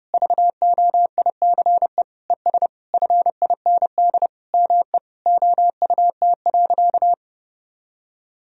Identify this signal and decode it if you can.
Morse code, heard as a keyed tone, about 30 wpm, 700 Hz. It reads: VOICE ES FIND ME OUT.
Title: Sentences from Top 500 Words and Abbreviations 30wpm